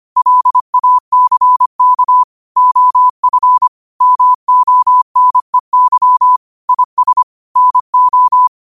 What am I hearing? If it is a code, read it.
Morse code, heard as a keyed tone, about 25 wpm, 1000 Hz. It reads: LACK OF MONEY IS NO
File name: qotdfiles / 25wpm